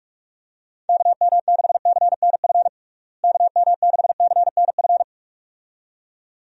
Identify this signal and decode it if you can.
Morse code, heard as a keyed tone, about 46 wpm, 700 Hz. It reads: KM6CNF KM6CNF
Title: USA Call Signs - 6 Characters - Repeated 2x - 45wpm